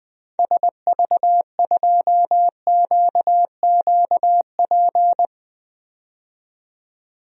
Morse code, heard as a keyed tone, about 20 wpm, 700 Hz. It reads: SV2QQP
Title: DXCC-100 Call Signs - 6 Characters - 20wpm